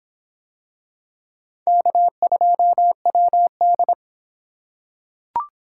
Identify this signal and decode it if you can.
Morse code, heard as a keyed tone, about 25 wpm, 700 Hz. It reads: K2WD